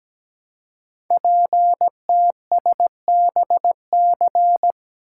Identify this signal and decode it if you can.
Morse code, heard as a keyed tone, about 17 wpm, 700 Hz. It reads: PTSBC
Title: Code Groups - 5 Letters - 17wpm